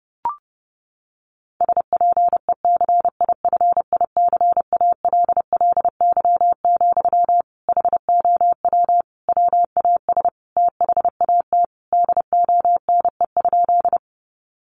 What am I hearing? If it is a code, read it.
Morse code, heard as a keyed tone, about 28 wpm, 700 Hz. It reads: SPECIFICALLY, HOW WAS THAT DONE?
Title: Sentences from Top 1100 Words 30wpm